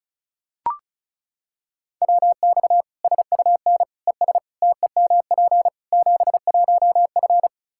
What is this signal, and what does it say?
Morse code, heard as a keyed tone, about 34 wpm, 700 Hz. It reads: WX SUN ES TEMP 71F
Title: Practice QSO - Exchange 2 - Info - 35wpm